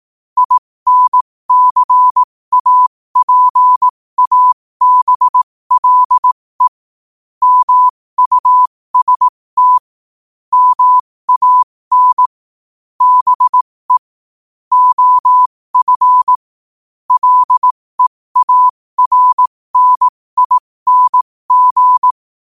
Morse code, heard as a keyed tone, about 18 wpm, spaced as for 15 wpm, 1000 Hz. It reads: INCAPABLE MUST MAN BE OF LEARNING
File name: qotdfiles / 15wpm